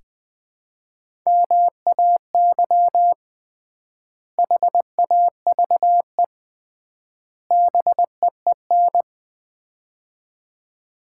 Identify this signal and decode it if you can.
Morse code, heard as a keyed tone, about 20 wpm, 700 Hz. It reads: MAY HAVE BEEN
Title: 3x Word-Spacing - Sets of 3 Words - 20wpm